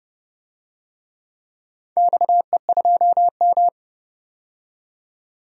Morse code, heard as a keyed tone, about 29 wpm, 700 Hz. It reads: XE2M
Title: DXCC-100 Call Signs - 4 Characters - 30wpm